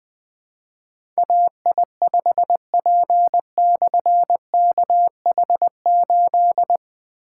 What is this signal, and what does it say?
Morse code, heard as a keyed tone, about 20 wpm, 700 Hz. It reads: AI5P/KH8
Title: Difficult Call Signs 20wpm